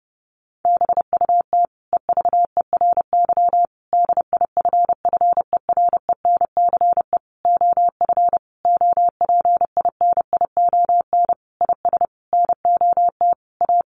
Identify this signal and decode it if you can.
Morse code, heard as a keyed tone, about 30 wpm, 700 Hz. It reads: BUT EVERY DIFFERENCE OF OPINION IS NOT A
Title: Sentences from Top 1400 Words 30wpm